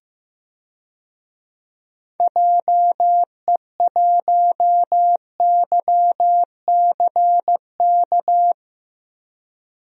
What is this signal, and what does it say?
Morse code, heard as a keyed tone, about 15 wpm, 700 Hz. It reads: JE1YCK
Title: DXCC-100 Call Signs - 6 Characters - Repeated 3x - 15wpm